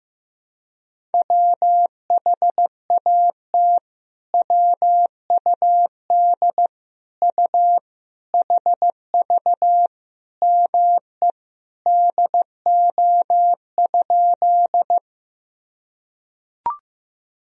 Morse code, heard as a keyed tone, about 16 wpm, 700 Hz. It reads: WHAT WUD U HV ME DO?
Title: Sentences from Top 200 Words and Abbreviations 15wpm